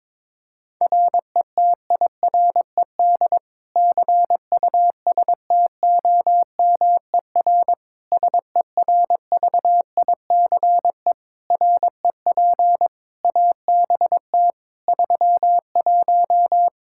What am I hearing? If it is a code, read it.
Morse code, heard as a keyed tone, about 22 wpm, 700 Hz. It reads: RETIRED CUSTOMER SERVICE REP ABT 31
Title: Practice QSO - Exchange 3 - More Info - 22wpm